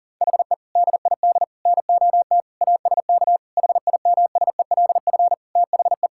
Morse code, heard as a keyed tone, about 40 wpm, 700 Hz. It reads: HE DID NOT ASK HIMSELF THE
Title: Sentences from Top 1600 Words 40wpm